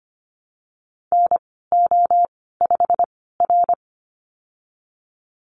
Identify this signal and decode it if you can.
Morse code, heard as a keyed tone, about 25 wpm, spaced as for 16 wpm, 700 Hz. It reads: NO5R
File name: Lesson 15g - Number 5 - Call Signs 25wpm